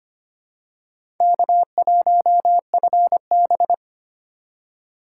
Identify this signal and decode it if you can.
Morse code, heard as a keyed tone, about 25 wpm, 700 Hz. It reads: K1FB
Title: USA Call Signs - 4 Characters - 2x Character Spacing - 25wpm